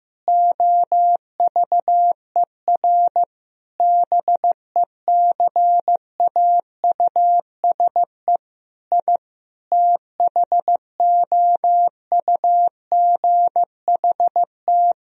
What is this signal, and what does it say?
Morse code, heard as a keyed tone, about 15 wpm, 700 Hz. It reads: OVER BECAUSE I THOUGHT
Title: Sentences from Top 1000 Words - Review Part 3 - 15wpm